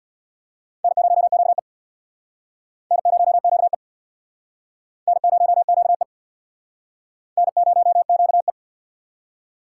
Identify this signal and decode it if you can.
Morse code, heard as a keyed tone, about 66 wpm, 700 Hz. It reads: N0QE N0QE N0QE N0QE